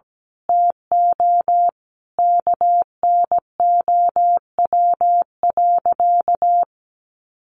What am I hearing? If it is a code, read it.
Morse code, heard as a keyed tone, about 17 wpm, 700 Hz. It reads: TO KNOW.
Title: Sentences from Top 1200 Words 17wpm